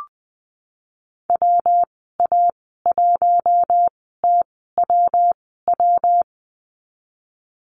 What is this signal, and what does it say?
Morse code, heard as a keyed tone, about 20 wpm, spaced as for 15 wpm, 700 Hz. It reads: WA1TWW
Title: 147 Lesson 24g - Letter y - Call Signs 20wpm